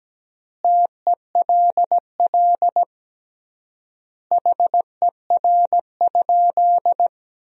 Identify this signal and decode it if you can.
Morse code, heard as a keyed tone, about 17 wpm, 700 Hz. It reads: TELL HER?
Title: 3x Word-Spacing - Sentences from Top 200 Words - 17wpm